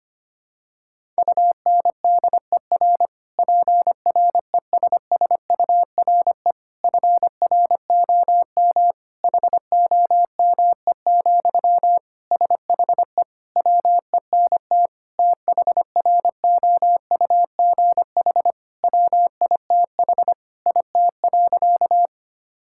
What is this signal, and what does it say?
Morse code, heard as a keyed tone, about 25 wpm, 700 Hz. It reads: UNDER PRESSURE FROM HOME, SHE WENT THROUGH WITH IT.
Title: Sentences from Top 1000 Words - Review Part 6 - 25wpm